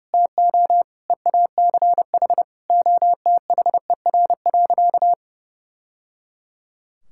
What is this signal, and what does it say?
Morse code, heard as a keyed tone, about 30 wpm, 700 Hz. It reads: TO EACH OTHER.
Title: Sentences from Top 1000 Words - Review Part 5 - 30wpm